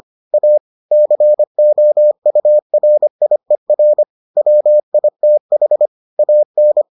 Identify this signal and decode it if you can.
Morse code, heard as a keyed tone, about 25 wpm, 600 Hz. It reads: A COURIER WITH AN